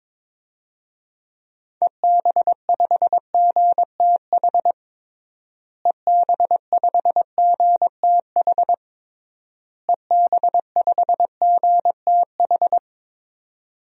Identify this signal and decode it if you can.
Morse code, heard as a keyed tone, about 22 wpm, 700 Hz. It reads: EB5GTH EB5GTH EB5GTH
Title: DXCC-100 Call Signs - 6 Characters - Repeated 3x - 22wpm